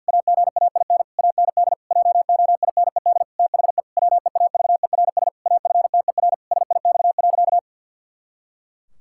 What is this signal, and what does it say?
Morse code, heard as a keyed tone, about 53 wpm, 700 Hz. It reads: AGAIN AND JOINED THE WEAVERS AFTER SIX.
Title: Pangram Challenge v2 - 50wpm